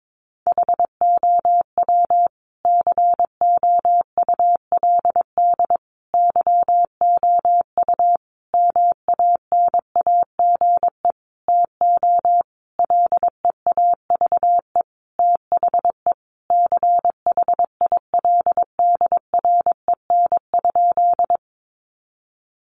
Morse code, heard as a keyed tone, about 22 wpm, 700 Hz. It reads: HOW COULD YOU MANAGE TO LEAVE THE CHILDREN?
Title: Sentences from Top 1400 Words 22wpm